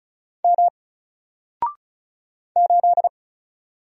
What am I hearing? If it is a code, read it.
Morse code, heard as a keyed tone, about 36 wpm, 700 Hz. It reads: M 8
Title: Single Character (Letter:Number:Punctuation:Prosign) Alt 35wpm